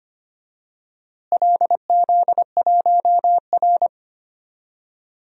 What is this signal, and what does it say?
Morse code, heard as a keyed tone, about 25 wpm, 700 Hz. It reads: LZ1R
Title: DXCC-100 Call Signs - 4 Characters - Repeated 2x - 25wpm